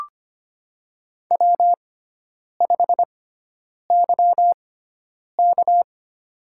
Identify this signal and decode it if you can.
Morse code, heard as a keyed tone, about 25 wpm, spaced as for 9 wpm, 700 Hz. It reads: W5YK
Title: USA Call Signs - 4 Characters - 6x Character Spacing - 25wpm